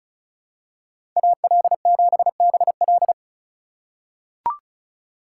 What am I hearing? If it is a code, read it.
Morse code, heard as a keyed tone, about 34 wpm, 700 Hz. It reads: AL7BL
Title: USA Call Signs - 5 Characters - 2x Character Spacing - 35wpm